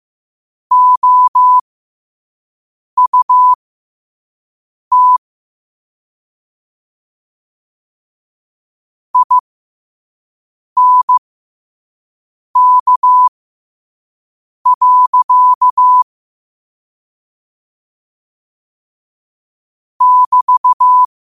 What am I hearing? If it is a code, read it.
Morse code, heard as a keyed tone, about 15 wpm, spaced as for 5 wpm, 1000 Hz. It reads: OUT INK. <BT>
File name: qotdfiles / 5wpm